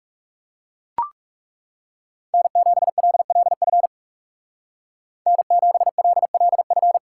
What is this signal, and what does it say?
Morse code, heard as a keyed tone, about 37 wpm, 700 Hz. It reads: N7LLF N7LLF
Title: Speed Racing - USA 1x3 Call Signs 30wpm